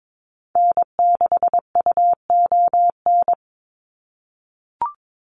Morse code, heard as a keyed tone, about 22 wpm, 700 Hz. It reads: N6UON
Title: USA Call Signs 22wpm